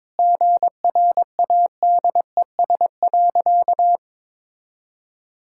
Morse code, heard as a keyed tone, about 22 wpm, 700 Hz. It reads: GRADES.
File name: practice / 22wpm / Sentences with 3-Word Phrases - 22wpm